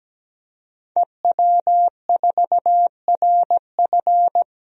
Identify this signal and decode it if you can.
Morse code, heard as a keyed tone, about 17 wpm, 700 Hz. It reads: EW4RF
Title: DXCC-100 Call Signs - 5 Characters - 17wpm